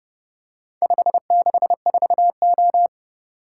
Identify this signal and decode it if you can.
Morse code, heard as a keyed tone, about 31 wpm, 700 Hz. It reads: 564O